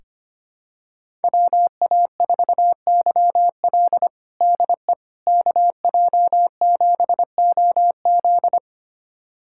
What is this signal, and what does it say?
Morse code, heard as a keyed tone, about 25 wpm, 700 Hz. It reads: WA4YL DE KJ7OZ
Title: USA Callsign DE Callsign 25wpm